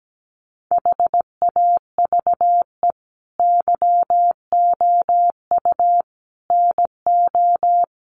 Morse code, heard as a keyed tone, about 17 wpm, 700 Hz. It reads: HAVE YOU NO
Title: Sentences from Top 1200 Words 17wpm